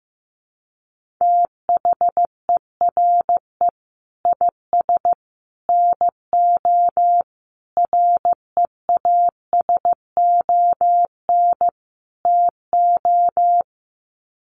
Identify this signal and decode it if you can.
Morse code, heard as a keyed tone, about 15 wpm, 700 Hz. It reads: THERE IS NO REASON TO
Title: Sets of 5 Words Encore 15wpm